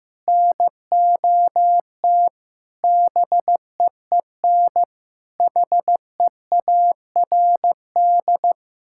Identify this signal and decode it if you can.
Morse code, heard as a keyed tone, about 16 wpm, 700 Hz. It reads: NOT BEEN HEARD
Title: Sentences from Top 1600 Words 15wpm